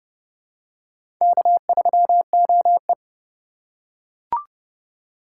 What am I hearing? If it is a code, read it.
Morse code, heard as a keyed tone, about 29 wpm, 700 Hz. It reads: K3OE